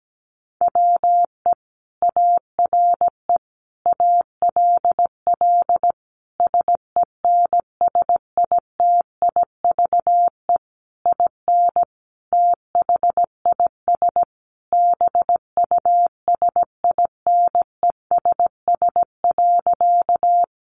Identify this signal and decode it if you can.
Morse code, heard as a keyed tone, about 17 wpm, 700 Hz. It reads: WE ARE ALL SENSITIVE IN THIS BUSINESS.
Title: Sentences from Top 1400 Words 17wpm